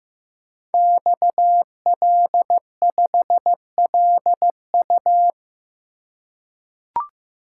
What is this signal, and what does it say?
Morse code, heard as a keyed tone, about 15 wpm, 700 Hz. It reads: XL5LU
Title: Code Groups - 5 Letter:Numbers - 15wpm